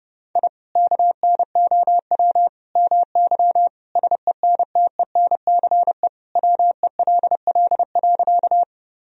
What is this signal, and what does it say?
Morse code, heard as a keyed tone, about 30 wpm, 700 Hz. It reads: I KNOW MY SENTENCE WELL.
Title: Sentences from Top 1000 Words - Review Part 1 - 30wpm